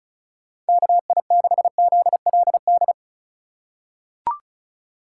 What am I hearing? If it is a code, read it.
Morse code, heard as a keyed tone, about 36 wpm, 700 Hz. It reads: KI6ZLD
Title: USA Call Signs - 4x Character Spacing - 35wpm